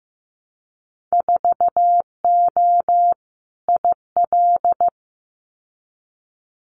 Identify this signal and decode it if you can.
Morse code, heard as a keyed tone, about 15 wpm, 700 Hz. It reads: 4O IL
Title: Winter Field Day Exchanges Encore 15wpm